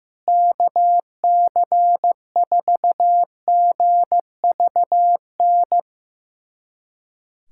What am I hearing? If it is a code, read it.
Morse code, heard as a keyed tone, about 15 wpm, 700 Hz. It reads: KC4GVN
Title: USA Callsign DE Callsign - 6 Characters - 15wpm